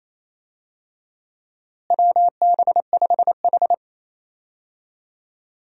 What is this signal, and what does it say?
Morse code, heard as a keyed tone, about 28 wpm, 700 Hz. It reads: WB5H
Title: USA Call Signs - 4 Characters - Repeated 3x - 28wpm